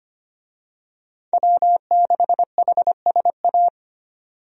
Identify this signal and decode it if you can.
Morse code, heard as a keyed tone, about 25 wpm, 700 Hz. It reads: W6HSA
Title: USA Call Signs - 5 Characters - 25wpm